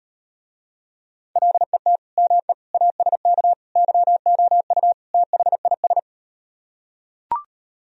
Morse code, heard as a keyed tone, about 38 wpm, 700 Hz. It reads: LET ME ASK YOU THIS